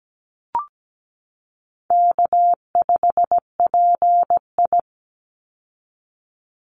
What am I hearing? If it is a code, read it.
Morse code, heard as a keyed tone, about 17 wpm, 700 Hz. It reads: K5PI
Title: Easy Call Signs Repeated 2x - 17wpm